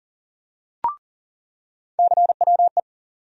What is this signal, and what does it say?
Morse code, heard as a keyed tone, about 40 wpm, 700 Hz. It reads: CWE
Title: USA Call Sign Postfixes - 3 Letters 40wpm